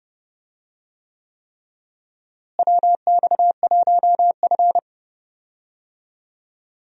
Morse code, heard as a keyed tone, about 31 wpm, 700 Hz. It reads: WX1F
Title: USA Call Signs - 4 Characters - Repeated 3x - 30wpm